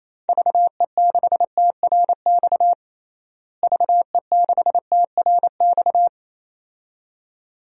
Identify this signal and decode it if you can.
Morse code, heard as a keyed tone, about 28 wpm, 700 Hz. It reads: VE6TRX VE6TRX
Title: DXCC-100 Call Signs - 6 Characters - Repeated 2x - 28wpm